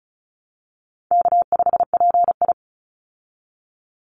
Morse code, heard as a keyed tone, about 34 wpm, 700 Hz. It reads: K5PI
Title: Easy Call Signs Repeated 3x - 35wpm